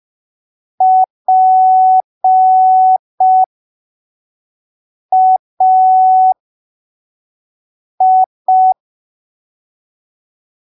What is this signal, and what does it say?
Morse code, heard as a keyed tone, about 5 wpm, spaced as for 3 wpm, 750 Hz. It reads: PAI
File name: arquivos / texto / texto 08 05ppm